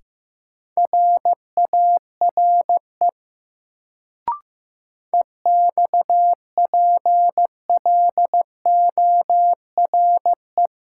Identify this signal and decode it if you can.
Morse code, heard as a keyed tone, about 15 wpm, 700 Hz. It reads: RARE EXPLORE
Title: Sentences from Top 1100 Words - New Word Drill - 15wpm